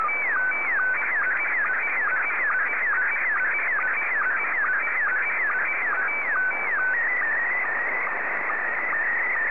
SSTV (slow-scan television), Scottie 1.